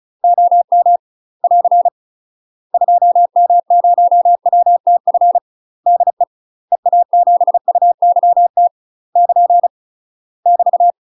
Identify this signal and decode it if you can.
Morse code, heard as a keyed tone, about 34 wpm, 700 Hz. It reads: OM <AR> 2M0WTF DE EA7UYT <KN> <BT>
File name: UK MASTER QSOs TRAINING / 35WPM / UK MASTER QSOs 0011